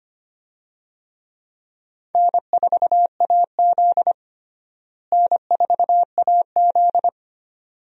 Morse code, heard as a keyed tone, about 25 wpm, 700 Hz. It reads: N4AZ N4AZ
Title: USA Call Signs - 4 Characters - Repeated 2x - 25wpm